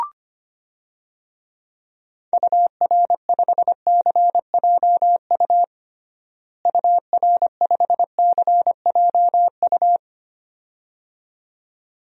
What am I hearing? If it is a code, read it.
Morse code, heard as a keyed tone, about 25 wpm, 700 Hz. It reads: UR5CJU UR5CJU
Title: DXCC-100 Call Signs - 6 Characters - Repeated 2x - 25wpm